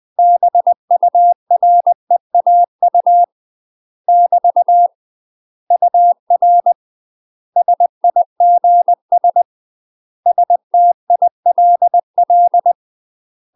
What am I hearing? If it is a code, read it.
Morse code, heard as a keyed tone, about 20 wpm, 700 Hz. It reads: BUREAU <BT> UR SIGS STILL
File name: UK MASTER QSOs TRAINING/20WPM